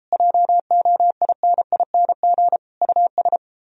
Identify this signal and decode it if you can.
Morse code, heard as a keyed tone, about 29 wpm, 700 Hz. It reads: JOINING US